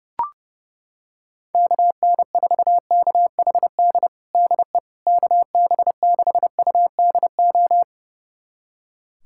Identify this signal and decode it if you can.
Morse code, heard as a keyed tone, about 32 wpm, 700 Hz. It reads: KN4KHD DE KB6UDO
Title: USA Callsign DE Callsign 30wpm